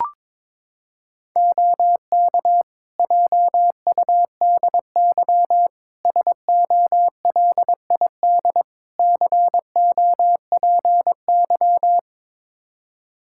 Morse code, heard as a keyed tone, about 22 wpm, 700 Hz. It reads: OK JUDY SOLID COPY
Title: Practice QSO - Exchange 3 - More Info - 22wpm